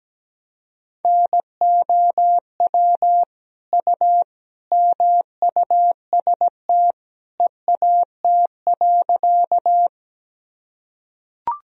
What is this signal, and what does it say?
Morse code, heard as a keyed tone, about 17 wpm, 700 Hz. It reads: NOW U MUST EAT.
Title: Sentences from Top 300 Words and Abbreviations 17wpm